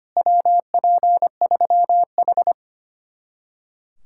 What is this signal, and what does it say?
Morse code, heard as a keyed tone, about 32 wpm, 700 Hz. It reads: WP3H